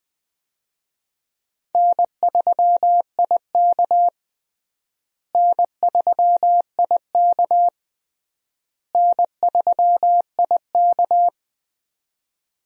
Morse code, heard as a keyed tone, about 20 wpm, 700 Hz. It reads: N3IK N3IK N3IK
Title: USA Call Signs - 4 Characters - Repeated 3x - 20wpm